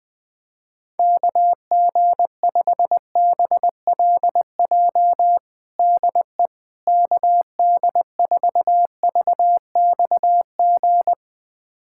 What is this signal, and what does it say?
Morse code, heard as a keyed tone, about 20 wpm, 700 Hz. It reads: KG5BLJ DE KD4VXG
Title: USA Callsign DE Callsign 20wpm